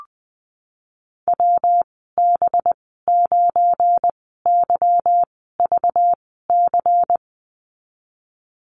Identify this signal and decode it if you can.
Morse code, heard as a keyed tone, about 20 wpm, spaced as for 15 wpm, 700 Hz. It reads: WB9YVC
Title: 200 Lesson 32g - Letter k - Call Signs 20wpm